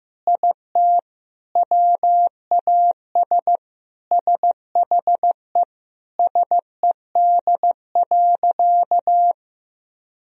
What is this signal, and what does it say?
Morse code, heard as a keyed tone, about 15 wpm, 700 Hz. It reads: IT WAS SHE SED.
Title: Sentences from Top 400 Words and Abbreviations 15wpm